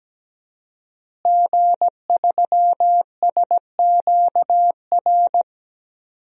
Morse code, heard as a keyed tone, about 17 wpm, 700 Hz. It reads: G3SQR
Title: DXCC-100 Call Signs - 5 Characters - Repeated 2x - 17wpm